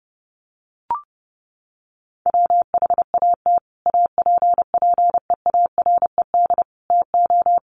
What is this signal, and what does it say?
Morse code, heard as a keyed tone, about 28 wpm, 700 Hz. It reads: WHAT APPEARED TO
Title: Sets of 5 Words Encore2 30wpm